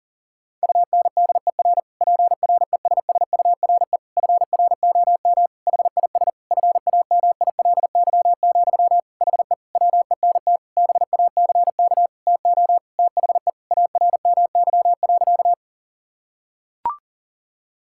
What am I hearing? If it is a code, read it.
Morse code, heard as a keyed tone, about 38 wpm, 700 Hz. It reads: UNDER PRESSURE FROM HIS FAMILY, HE WENT BACK TO THE ARMY.
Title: Sentences from Top 1000 Words - Review Part 6 - 40wpm